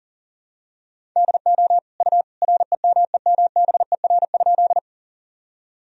Morse code, heard as a keyed tone, about 40 wpm, 700 Hz. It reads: DO U REMEMBER?
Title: Sentences from Top 400 Words and Abbreviations 40wpm